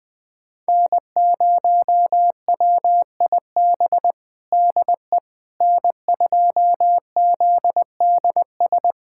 Morse code, heard as a keyed tone, about 20 wpm, 700 Hz. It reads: N0WIB DE N2ZDS
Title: USA Callsign DE Callsign - 5 Characters - 20wpm